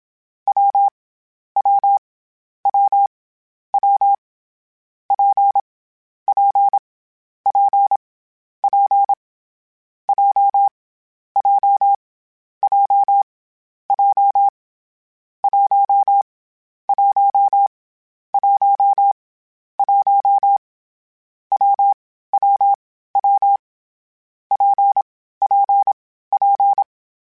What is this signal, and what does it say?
Morse code, heard as a keyed tone, about 27 wpm, 800 Hz. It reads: W W W W P P P P J J J J 1 1 1 1 W W W P P P
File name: Variable Speed / Variable Speed-mp3s / WPJ WPJ1 sp @25